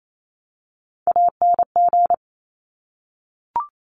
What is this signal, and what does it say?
Morse code, heard as a keyed tone, about 28 wpm, 700 Hz. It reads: ANG
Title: Top Three Letter Combinations in Words 28wpm